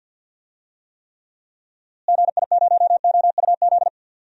Morse code, heard as a keyed tone, about 49 wpm, 700 Hz. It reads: GI0OUZ
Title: DXCC-100 Call Signs - 6 Characters - Repeated 3x - 50wpm